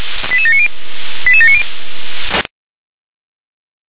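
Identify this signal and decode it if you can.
Five-tone selective call (ZVEI), heard as quick runs of five tones, tones of about 70 ms. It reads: 88790 88790
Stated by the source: ZVEI five-tone selective call 88727, received by SDR.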